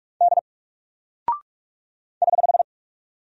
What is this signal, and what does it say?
Morse code, heard as a keyed tone, about 44 wpm, 700 Hz. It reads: D <HH>
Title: Single Character (Letter:Number:Punctuation:Prosign) 45wpm